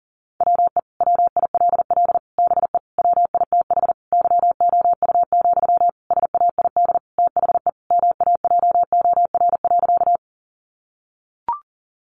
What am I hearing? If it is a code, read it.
Morse code, heard as a keyed tone, about 40 wpm, 700 Hz. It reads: WE WILL BE WITH YOU, SAID THE MAJOR.
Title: Sentences from Top 600 Words 40wpm